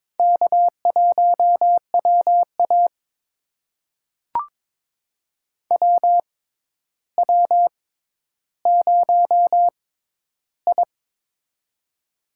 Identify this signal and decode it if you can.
Morse code, heard as a keyed tone, about 22 wpm, spaced as for 11 wpm, 700 Hz. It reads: K1WA WW0I
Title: USA Call Signs - 4 Characters - 6x Character Spacing - 22wpm